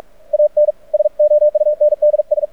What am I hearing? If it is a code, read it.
Morse code, heard as a keyed tone, about 44 wpm, 600 Hz. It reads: AN SOUNDS
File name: archived / CRSnet